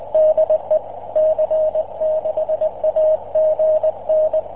Morse code, heard as a keyed tone, about 20 wpm, 620 Hz. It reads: DE C6AGN